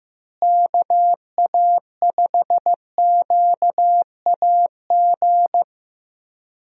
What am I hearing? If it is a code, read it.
Morse code, heard as a keyed tone, about 15 wpm, 700 Hz. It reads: KA5QAG
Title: USA Call Signs - 6 Characters - Repeated 3x - 15wpm